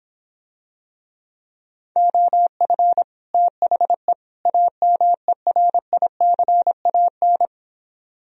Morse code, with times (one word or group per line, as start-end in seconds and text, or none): 2.0-3.0 OF
3.3-4.1 THE
4.5-7.5 AMERICAN